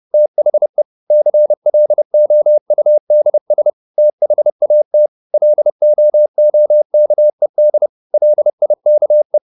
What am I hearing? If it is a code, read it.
Morse code, heard as a keyed tone, about 30 wpm, 600 Hz. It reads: THE CLOUDS THAT LOOKED LIKE
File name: the old man and the sea / the old man and the sea 1 30wpm